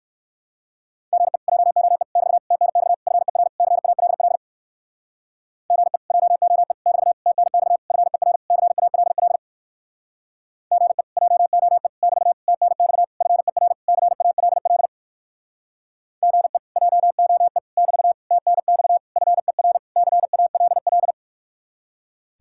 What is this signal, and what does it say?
Morse code, heard as a keyed tone, about 64 wpm, 700 Hz. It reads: GE JOE <BT> TNX FER CALL GE JOE <BT> TNX FER CALL GE JOE <BT> TNX FER CALL GE JOE <BT> TNX FER CALL